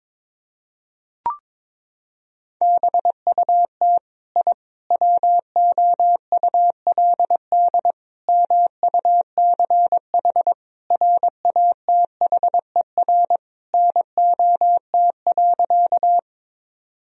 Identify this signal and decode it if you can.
Morse code, heard as a keyed tone, about 21 wpm, 700 Hz. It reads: BUT I WOULD MUCH RATHER NOT.
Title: Sentences from Top 1000 Words - Review Part 4 - 22wpm